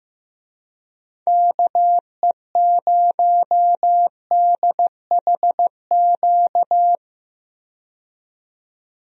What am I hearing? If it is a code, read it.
Morse code, heard as a keyed tone, about 15 wpm, 700 Hz. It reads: KE0DHQ
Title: USA Call Signs - 6 Characters - 15wpm